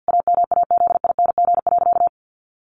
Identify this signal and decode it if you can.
Morse code, heard as a keyed tone, about 50 wpm, 700 Hz. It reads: AMAZING.
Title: Sentences from Top 1500 Words 50wpm